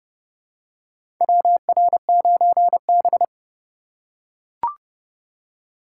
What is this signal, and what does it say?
Morse code, heard as a keyed tone, about 32 wpm, 700 Hz. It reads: WR9B
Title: USA Call Signs - 4 Characters - 3x Character Spacing - 30wpm